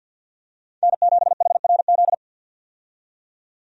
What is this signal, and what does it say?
Morse code, heard as a keyed tone, about 46 wpm, 700 Hz. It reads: N7HRZ
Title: USA Call Signs - 5 Characters - 50wpm